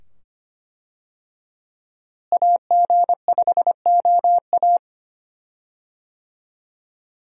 Morse code, heard as a keyed tone, about 25 wpm, 700 Hz. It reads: AG5OA
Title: USA Call Signs - 5 Characters - Repeated 2x - 25wpm